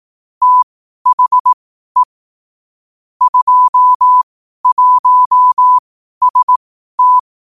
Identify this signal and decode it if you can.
Morse code, heard as a keyed tone, about 18 wpm, spaced as for 12 wpm, 1000 Hz. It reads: THE 21ST
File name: 12wpm